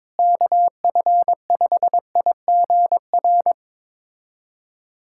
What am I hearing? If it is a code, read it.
Morse code, heard as a keyed tone, about 22 wpm, 700 Hz. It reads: KF5IGR
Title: USA Callsign DE Callsign - 6 Characters - 22wpm